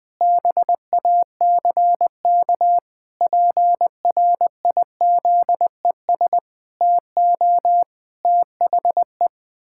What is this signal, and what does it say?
Morse code, heard as a keyed tone, about 20 wpm, 700 Hz. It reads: BACK PRIZES TO THE
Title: Pangram Challenge v2 - 20wpm